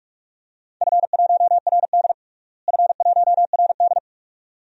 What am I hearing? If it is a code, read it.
Morse code, heard as a keyed tone, about 46 wpm, 700 Hz. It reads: F1RD F1RD
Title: DXCC-100 Call Signs - 4 Characters - Repeated 2x - 45wpm